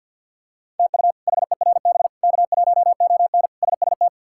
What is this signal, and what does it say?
Morse code, heard as a keyed tone, about 52 wpm, 700 Hz. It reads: TU HERB K1ON SST
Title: K1USN SST QSO Practice - 50wpm